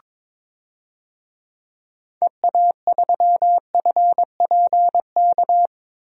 Morse code, heard as a keyed tone, about 22 wpm, 700 Hz. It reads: EA3FPK
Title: DXCC-100 Call Signs - 6 Characters - Repeated 2x - 22wpm